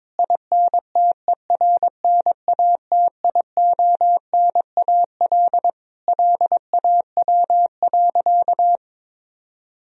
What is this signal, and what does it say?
Morse code, heard as a keyed tone, about 22 wpm, 700 Hz. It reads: INTERNATIONAL LAW.
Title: Sentences from Top 1000 Words - Review Part 4 - 22wpm